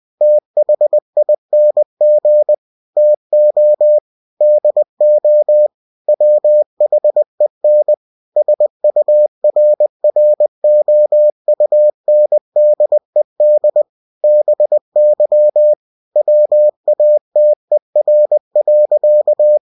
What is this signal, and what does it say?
Morse code, heard as a keyed tone, about 20 wpm, 600 Hz. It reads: THING TO DO WHEN SURROUNDED BY WATER.
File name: cw-books / winnie-the-pooh / winnie-20wpm-page-0091